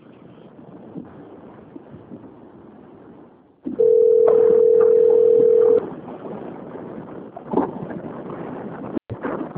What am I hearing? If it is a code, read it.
Ringback tone (North America), 440 + 480 Hz.
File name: out- - - - - .30038